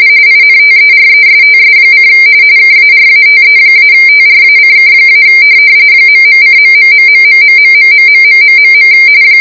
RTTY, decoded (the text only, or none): INTERESTS: KARATE-TRACK- ..:..............